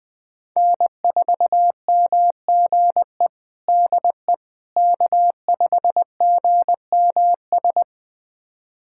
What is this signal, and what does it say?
Morse code, heard as a keyed tone, about 20 wpm, 700 Hz. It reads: N4MGE DE K5GMS
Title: USA Callsign DE Callsign - 5 Characters - 20wpm